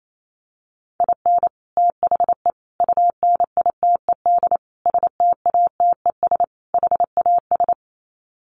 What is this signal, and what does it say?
Morse code, heard as a keyed tone, about 28 wpm, 700 Hz. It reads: IN THE UNITED STATES HAS
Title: Sets of 5 Words Encore2 28wpm